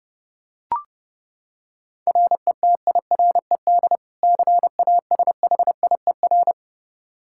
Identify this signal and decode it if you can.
Morse code, heard as a keyed tone, about 32 wpm, 700 Hz. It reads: RETIRED CASHIER
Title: Practice QSO - Exchange 3 - More Info - 30wpm